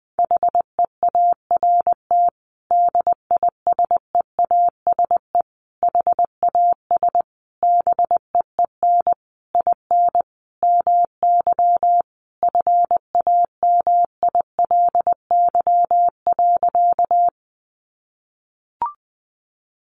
Morse code, heard as a keyed tone, about 20 wpm, 700 Hz. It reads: HEART DISEASE HAS BEEN IN MY FAMILY.
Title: Sentences from Top 1200 Words 20wpm